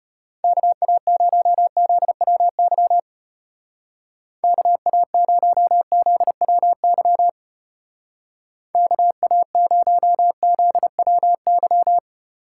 Morse code, heard as a keyed tone, about 31 wpm, 700 Hz. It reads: KA0ZWY KA0ZWY KA0ZWY
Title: Speed Racing - USA Call Signs 25wpm